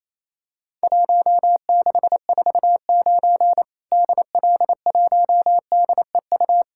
Morse code, heard as a keyed tone, about 27 wpm, 700 Hz. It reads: 1649 DL1DEU
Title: Worked All Europe DX Contest - QTC Exchange - 28wpm